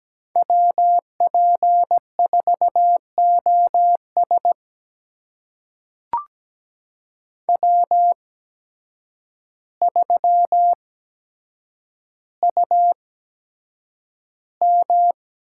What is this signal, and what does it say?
Morse code, heard as a keyed tone, about 17 wpm, spaced as for 8 wpm, 700 Hz. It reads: WP4OS W3UM
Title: USA Call Signs - 5 Characters - 8x Character Spacing - 17wpm